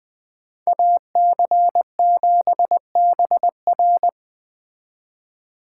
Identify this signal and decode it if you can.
Morse code, heard as a keyed tone, about 21 wpm, 700 Hz. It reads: AC7BR
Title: Speed Racing - USA 2x2 Call Signs 17wpm